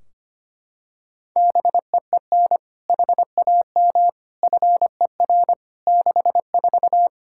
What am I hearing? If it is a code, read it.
Morse code, heard as a keyed tone, about 25 wpm, 700 Hz. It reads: BEEN HAM FER 64
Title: K1USN SST QSO Practice 25wpm